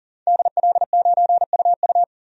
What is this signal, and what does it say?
Morse code, heard as a keyed tone, about 40 wpm, 700 Hz. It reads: DL9UU
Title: DXCC-100 Call Signs - 5 Characters - 40wpm